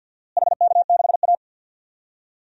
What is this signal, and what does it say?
Morse code, heard as a keyed tone, about 52 wpm, 700 Hz. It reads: HK6A
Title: DXCC-100 Call Signs - 4 Characters - Repeated 2x - 50wpm